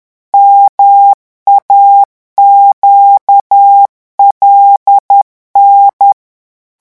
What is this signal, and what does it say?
Morse code, heard as a keyed tone, about 10 wpm, 800 Hz. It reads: MAQLN